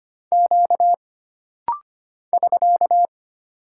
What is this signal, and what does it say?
Morse code, heard as a keyed tone, about 25 wpm, 700 Hz. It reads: Q <SK>
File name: Single Character (Letter:Number:Punctuation:Prosign) - Rapid-Fire - 25wpm